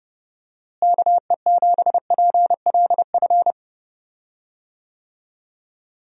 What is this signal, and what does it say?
Morse code, heard as a keyed tone, about 28 wpm, 700 Hz. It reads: KE7PLF
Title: USA Call Signs - 6 Characters - Repeated 3x - 30wpm